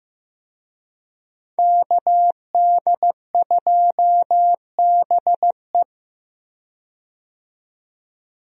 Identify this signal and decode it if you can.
Morse code, heard as a keyed tone, about 15 wpm, 700 Hz. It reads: KD2BE
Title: USA Call Signs - 5 Characters - 15wpm